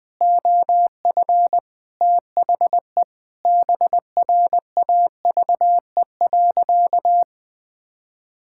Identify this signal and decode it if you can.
Morse code, heard as a keyed tone, about 20 wpm, 700 Hz. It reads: OF THE BRAVE.
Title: Sentences with 9-Word Phrases - 20wpm